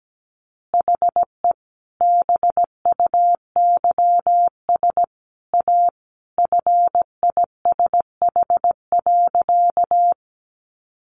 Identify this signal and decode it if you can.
Morse code, heard as a keyed tone, about 17 wpm, 700 Hz. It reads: HE BUYS A FISH.
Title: Additional Easy Phrases 17wpm